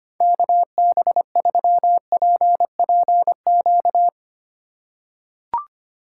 Morse code, heard as a keyed tone, about 25 wpm, 700 Hz. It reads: KB3PPQ
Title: USA Call Signs - 6 Characters - 2x Character Spacing - 25wpm